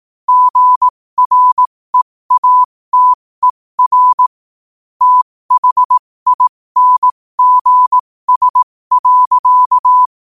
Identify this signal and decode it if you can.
Morse code, heard as a keyed tone, about 18 wpm, spaced as for 15 wpm, 1000 Hz. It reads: GREATER THINGS.
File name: qotdfiles / 15wpm